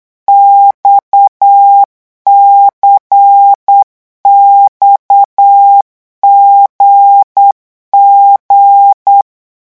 Morse code, heard as a keyed tone, about 8 wpm, 800 Hz. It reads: XCXGG
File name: morse mp3 / revision3-vitesse 8 mots